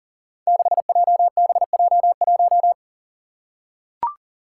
Morse code, heard as a keyed tone, about 40 wpm, 700 Hz. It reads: 6JBJ1
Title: Five Challenging Letter-Number Combinations 40wpm